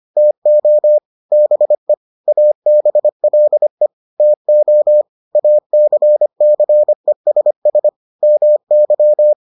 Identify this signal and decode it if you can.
Morse code, heard as a keyed tone, about 25 wpm, 600 Hz. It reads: TO BE ABLE TO ACCESS MY